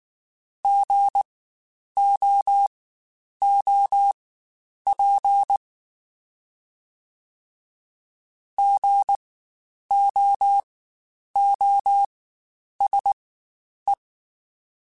Morse code, heard as a keyed tone, about 19 wpm, spaced as for 9 wpm, 800 Hz. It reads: GOOP GOOSE
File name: Spaced Alphas / Spaced Alphas mp3s / Gs sp @18